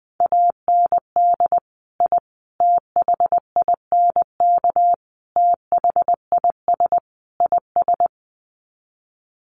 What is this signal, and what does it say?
Morse code, heard as a keyed tone, about 20 wpm, 700 Hz. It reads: AND I THINK THIS IS
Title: Sets of 5 Words Encore 20wpm